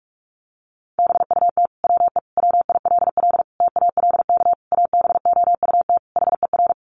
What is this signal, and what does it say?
Morse code, heard as a keyed tone, about 44 wpm, 700 Hz. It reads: BUT WE WILL TALK ABOUT HER
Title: Sentences from Top 1200 Words 45wpm